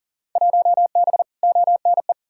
Morse code, heard as a keyed tone, about 40 wpm, 700 Hz. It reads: 1B ONE
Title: Field Day Exchanges - 40wpm